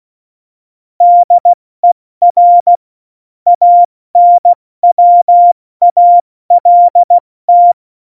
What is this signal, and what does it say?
Morse code, heard as a keyed tone, about 16 wpm, spaced as for 14 wpm, 700 Hz. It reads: DER ANWALT
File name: Morsekurs / Witze / Witz 18. Februar 2020, 70 F80